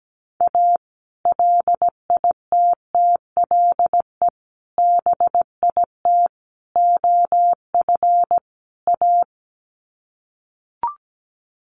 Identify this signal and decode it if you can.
Morse code, heard as a keyed tone, about 17 wpm, 700 Hz. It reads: A LITTLE BIT OF A
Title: Sets of 5 Words 17wpm